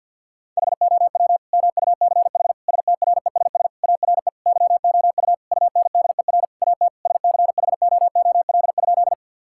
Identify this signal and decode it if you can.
Morse code, heard as a keyed tone, about 50 wpm, 700 Hz. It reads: HOW MUCH STRESS ARE YOU UNDER AT SCHOOL?
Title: Sentences from Top 1000 Words - Review Part 6 - 50wpm